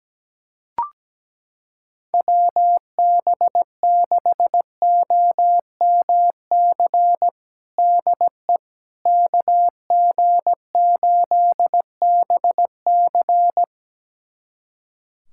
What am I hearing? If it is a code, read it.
Morse code, heard as a keyed tone, about 17 wpm, 700 Hz. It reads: WB6OMC DE KG8BC
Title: USA Callsign DE Callsign 17wpm